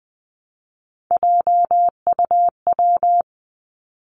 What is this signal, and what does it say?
Morse code, heard as a keyed tone, about 20 wpm, 700 Hz. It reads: JUW
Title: Three Challenging Letter-Number Combinations 20wpm